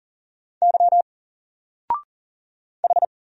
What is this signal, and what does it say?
Morse code, heard as a keyed tone, about 40 wpm, 700 Hz. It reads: Y H
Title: Single Character (Letter:Number:Punctuation:Prosign) 40wpm